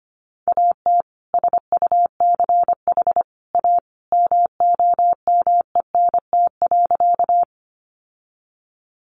Morse code, heard as a keyed tone, about 25 wpm, 700 Hz. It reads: AT SUCH A MOMENT.
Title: Sentences from Top 1000 Words 25wpm